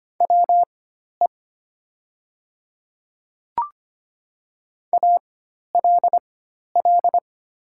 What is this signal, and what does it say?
Morse code, heard as a keyed tone, about 25 wpm, spaced as for 12 wpm, 700 Hz. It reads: WE ALL
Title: 4x Character Spacing - Top 300 Words - 25wpm